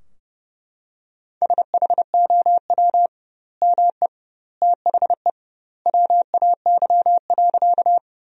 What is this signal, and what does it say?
Morse code, heard as a keyed tone, about 30 wpm, 700 Hz. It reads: SHOW ME THE WAY.
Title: 2x Word-Spacing - Sentences from Top 200 Words - 30wpm